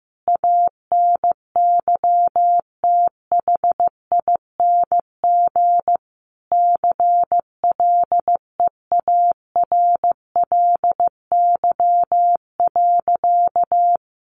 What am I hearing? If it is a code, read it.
Morse code, heard as a keyed tone, about 15 wpm, 700 Hz. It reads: ANYTHING CLEARLY.
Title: Sentences from Top 1300 Words 15wpm